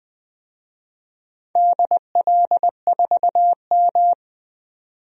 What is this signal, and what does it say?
Morse code, heard as a keyed tone, about 20 wpm, 700 Hz. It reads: DL4M
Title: DXCC-100 Call Signs - 4 Characters - 20wpm